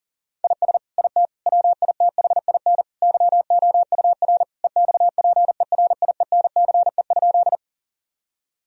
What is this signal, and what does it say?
Morse code, heard as a keyed tone, about 40 wpm, 700 Hz. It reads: IS IT WITHIN YOUR EXPERIENCE?
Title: Sentences from Top 1000 Words - Review Part 4 - 40wpm